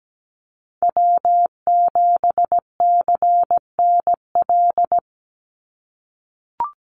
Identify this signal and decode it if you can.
Morse code, heard as a keyed tone, about 17 wpm, 700 Hz. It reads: W7CNL
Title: NA SOTA Chaser Callsigns 17wpm